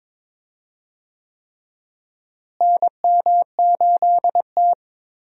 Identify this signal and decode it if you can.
Morse code, heard as a keyed tone, about 22 wpm, 700 Hz. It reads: NM8T
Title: USA Call Signs - 4 Characters - Repeated 3x - 22wpm